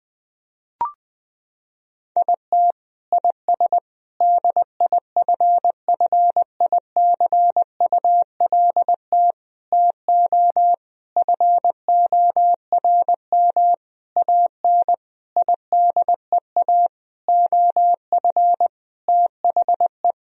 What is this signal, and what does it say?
Morse code, heard as a keyed tone, about 20 wpm, 700 Hz. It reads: IT IS DIFFICULT TO FORM AN IDEA OF THE
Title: Sentences from Top 1000 Words - Review Part 1 - 20wpm